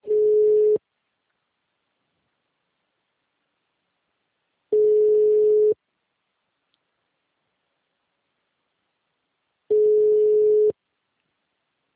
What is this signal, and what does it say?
Ringback tone (Europe, 425 Hz).